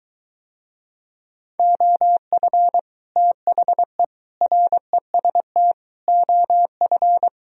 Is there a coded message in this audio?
Yes: Morse code from 1.6 to 7.3 s, heard as a keyed tone, about 23 wpm, 700 Hz. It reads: OF THE REST OF